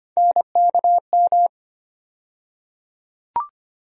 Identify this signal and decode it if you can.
Morse code, heard as a keyed tone, about 25 wpm, 700 Hz. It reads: NKM
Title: USA Call Sign Postfixes - 3 Letters 25wpm